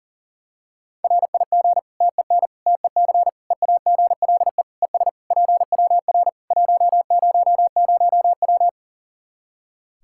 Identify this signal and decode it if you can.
Morse code, heard as a keyed tone, about 40 wpm, 700 Hz. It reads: RIG TEN TEC EAGLE ES PWR 100W
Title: Practice QSO - Exchange 2 - Info - 40wpm